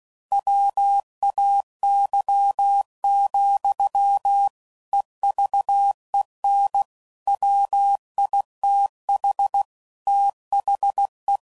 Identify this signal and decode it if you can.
Morse code, heard as a keyed tone, about 16 wpm, 800 Hz. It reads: WAY, EVEN WITH THE
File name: CW Code Groups @15WPM / Technical Subjects / OR Gate